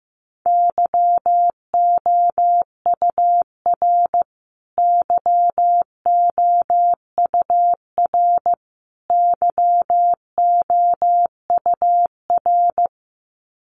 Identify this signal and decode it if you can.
Morse code, heard as a keyed tone, about 15 wpm, 700 Hz. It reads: YOUR YOUR YOUR
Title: Top 500 Words Repeated 15wpm